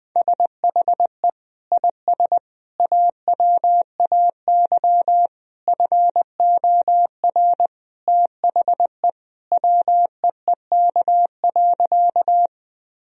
Morse code, heard as a keyed tone, about 20 wpm, 700 Hz. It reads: SHE IS AWAY FOR THE WEEK.
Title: Sentences from Top 1000 Words - Review Part 2 - 20wpm